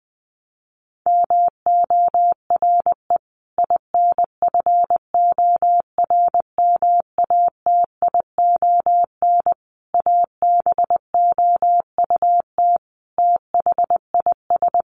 Morse code, heard as a keyed tone, about 20 wpm, 700 Hz. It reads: MORE INFORMATION ABOUT THIS
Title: Sentences from Top 1200 Words 20wpm